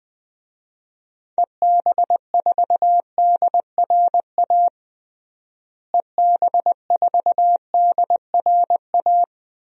Morse code, heard as a keyed tone, about 20 wpm, 700 Hz. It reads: EB4DRA EB4DRA
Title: DXCC-100 Call Signs - 6 Characters - Repeated 3x - 20wpm